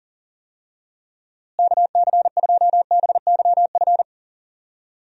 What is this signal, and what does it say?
Morse code, heard as a keyed tone, about 40 wpm, 700 Hz. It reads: KC2BYF
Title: USA Call Signs - 6 Characters - Repeated 3x - 40wpm